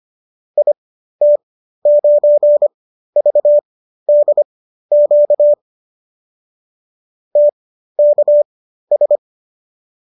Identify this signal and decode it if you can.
Morse code, heard as a keyed tone, about 25 wpm, spaced as for 13 wpm, 570 Hz. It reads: IT9VDQ TKS